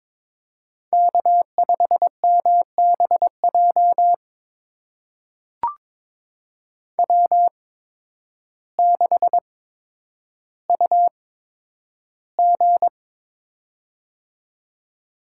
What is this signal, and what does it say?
Morse code, heard as a keyed tone, about 22 wpm, spaced as for 10 wpm, 700 Hz. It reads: K5MBJ W6UG
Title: USA Call Signs - 8x Character Spacing - 22wpm